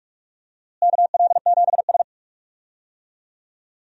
Morse code, heard as a keyed tone, about 46 wpm, 700 Hz. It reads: KL7S
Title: DXCC-100 Call Signs - 4 Characters - Repeated 2x - 45wpm